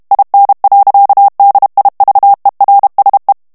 Morse code, heard as a keyed tone, about 32 wpm, 800 Hz. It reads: IN. DIVERSE
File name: HOLLANDAIS texte clair 2 vitesse 30 mots minute